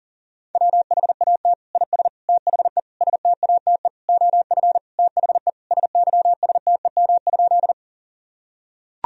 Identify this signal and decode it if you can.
Morse code, heard as a keyed tone, about 40 wpm, 700 Hz. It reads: WHAT IS THE STATE OF THE SYSTEM?
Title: Sentences from Top 1000 Words - Review Part 3 - 40wpm